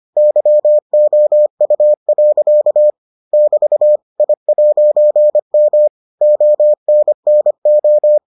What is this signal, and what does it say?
Morse code, heard as a keyed tone, about 25 wpm, 600 Hz. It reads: YOU. <BT> I'M ONNO